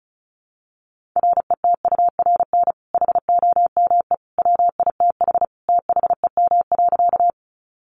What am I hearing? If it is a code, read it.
Morse code, heard as a keyed tone, about 34 wpm, 700 Hz. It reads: RETURN HOME WITH THEM.
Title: Sentences from Top 800 Words 35wpm